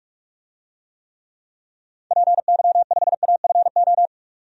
Morse code, heard as a keyed tone, about 47 wpm, 700 Hz. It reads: PY5AFO
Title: DXCC-100 Call Signs - 6 Characters - 45wpm